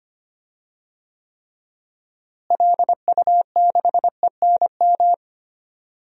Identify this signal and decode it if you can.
Morse code, heard as a keyed tone, about 25 wpm, 700 Hz. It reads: LU6ENM
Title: DXCC-100 Call Signs - 6 Characters - Repeated 3x - 25wpm